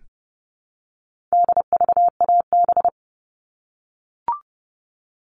Morse code, heard as a keyed tone, about 30 wpm, 700 Hz. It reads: DVAB
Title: All Cut Numbers - 4-Digit 30wpm